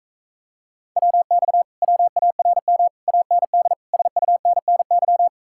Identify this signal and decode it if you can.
Morse code, heard as a keyed tone, about 45 wpm, 700 Hz. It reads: WX WARM AND SUNNY